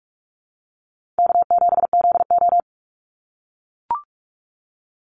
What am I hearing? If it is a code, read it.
Morse code, heard as a keyed tone, about 44 wpm, 690 Hz. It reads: K7ZO
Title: NA SOTA Chaser Callsigns 45wpm